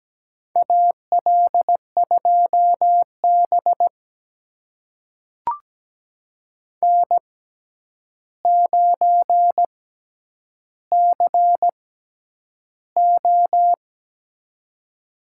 Morse code, heard as a keyed tone, about 17 wpm, spaced as for 9 wpm, 700 Hz. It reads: AL2B N9CO
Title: USA Call Signs - 4 Characters - 6x Character Spacing - 17wpm